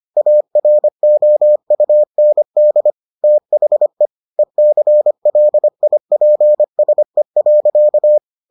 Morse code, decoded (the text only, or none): AROUND THE ECLIPSE.